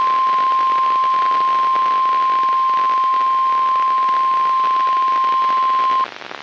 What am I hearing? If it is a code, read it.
Emergency Alert System (SAME) tones (the 1050 Hz attention tone), with noise. It is heard from 0.4 to 5.5 s.
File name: eas messages / EAS WXL38 Binghamton